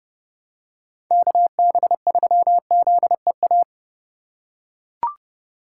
Morse code, heard as a keyed tone, about 30 wpm, 700 Hz. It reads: KB3ZEA
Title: USA Call Signs - 2x Character Spacing - 30wpm